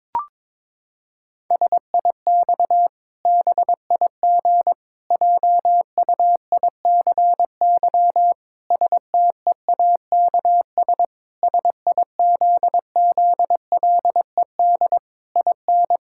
Morse code, heard as a keyed tone, about 22 wpm, 700 Hz. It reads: SIX BIG JUICY STEAKS SIZZLED IN
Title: Pangram Challenge v2 - 22wpm